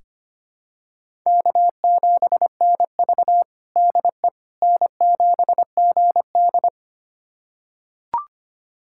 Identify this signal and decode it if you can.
Morse code, heard as a keyed tone, about 25 wpm, 700 Hz. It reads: K7NV DE N7GD
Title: USA Callsign DE Callsign - 4 Characters - 25wpm